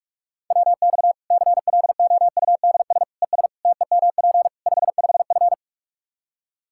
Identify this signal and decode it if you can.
Morse code, heard as a keyed tone, about 46 wpm, 700 Hz. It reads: WX CLOUDS ES TEMP 55F
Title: Practice QSO - Exchange 2 - Info - 45wpm